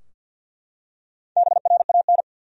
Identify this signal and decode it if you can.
Morse code, heard as a keyed tone, about 53 wpm, 700 Hz. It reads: BRAN
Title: USA Common Names - Max 4 Letters - 50wpm